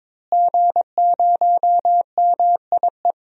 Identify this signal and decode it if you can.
Morse code, heard as a keyed tone, about 22 wpm, 700 Hz. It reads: G0MIE
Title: DXCC-100 Call Signs - 5 Characters - Repeated 3x - 22wpm